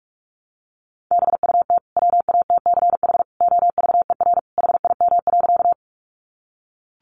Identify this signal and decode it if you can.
Morse code, heard as a keyed tone, about 44 wpm, 700 Hz. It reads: BUT WATCH OVER HIM.
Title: Sentences from Top 300 Words 45wpm